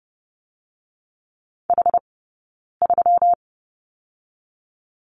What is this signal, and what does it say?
Morse code, heard as a keyed tone, about 30 wpm, 700 Hz. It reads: H 3
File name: Lesson 20d - Number 3 - Two Characters 30wpm